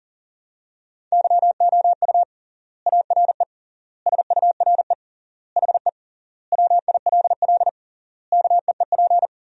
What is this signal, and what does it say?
Morse code, heard as a keyed tone, about 40 wpm, 700 Hz. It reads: YOU ARE SURE HE WILL KEEP
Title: 3x Word-Spacing - Sentences from Top 300 Words - 40wpm